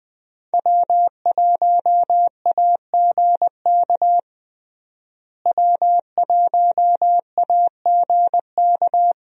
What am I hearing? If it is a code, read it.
Morse code, heard as a keyed tone, about 20 wpm, 700 Hz. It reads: W1AGK W1AGK
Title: USA Call Signs - 5 Characters - Repeated 2x - 20wpm